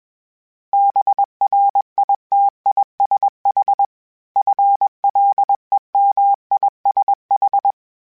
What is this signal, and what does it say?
Morse code, heard as a keyed tone, about 21 wpm, 800 Hz. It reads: BRITISH FLEMISH
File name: General / General-mp3s / ISH @20